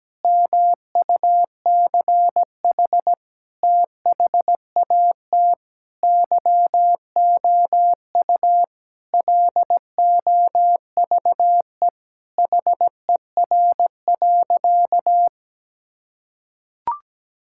Morse code, heard as a keyed tone, about 17 wpm, 700 Hz. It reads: MUCH THAT YOU LOVE HER.
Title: Sentences from Top 1600 Words 17wpm